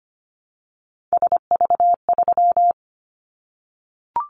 Morse code, heard as a keyed tone, about 25 wpm, 700 Hz. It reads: SV3
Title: Callsign Prefixes for Top 50 DXCC - 25wpm